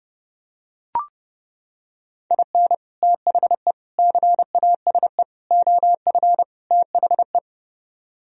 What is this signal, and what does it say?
Morse code, heard as a keyed tone, about 32 wpm, 700 Hz. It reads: IN THE CASE OF THE